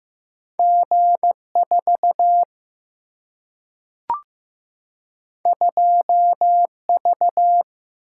Morse code, heard as a keyed tone, about 15 wpm, 700 Hz. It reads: G 4 2 V
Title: Single-Letter Single-Number 15wpm